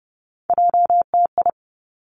Morse code, heard as a keyed tone, about 30 wpm, 700 Hz. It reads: JTI
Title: Three Challenging Letter-Number Combinations 30wpm